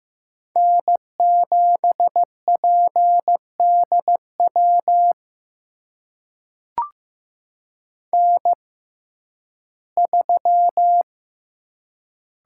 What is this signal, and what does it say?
Morse code, heard as a keyed tone, about 15 wpm, spaced as for 11 wpm, 700 Hz. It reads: N7PDW N3
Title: USA Call Signs - 5 Characters - 6x Character Spacing - 15wpm